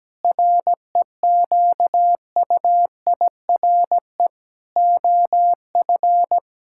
Morse code, heard as a keyed tone, about 17 wpm, 700 Hz. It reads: REQUIRE OF
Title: Sentences from Top 1000 Words - Review Part 6 - 17wpm